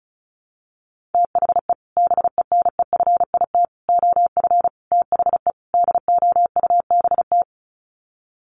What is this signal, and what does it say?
Morse code, heard as a keyed tone, about 34 wpm, 700 Hz. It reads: THE BENEFIT OF THE DOUBT
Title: Sets of 5 Words Encore 35wpm